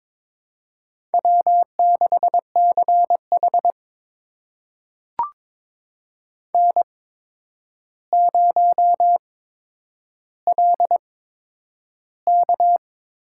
Morse code, heard as a keyed tone, about 22 wpm, spaced as for 9 wpm, 700 Hz. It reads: W6CH N0LK
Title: USA Call Signs - 4 Characters - 8x Character Spacing - 22wpm